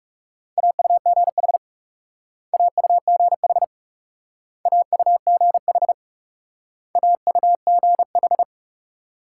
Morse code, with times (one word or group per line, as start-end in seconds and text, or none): 0.6-1.6 AUGH
2.5-3.6 AUGH
4.7-5.9 AUGH
7.0-8.4 AUGH